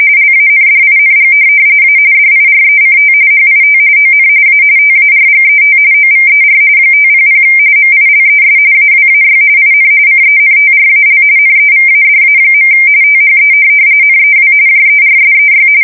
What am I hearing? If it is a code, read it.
RTTY, about 45.45 baud. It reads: LANDING A DIRECT HIT ON A TARGET OVER 500 MILES AWAY IN CENTRAL NEVADA. THE NEW JERSEY
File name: fsk170 / USS-New-Jersey